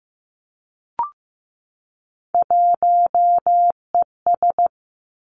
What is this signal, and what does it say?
Morse code, heard as a keyed tone, about 15 wpm, 700 Hz. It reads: 1ES
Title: Three Challenging Letter-Number Combinations 15wpm